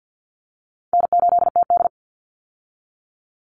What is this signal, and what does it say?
Morse code, heard as a keyed tone, about 48 wpm, 700 Hz. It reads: N8TD
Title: Easy Call Signs Repeated 2x - 50wpm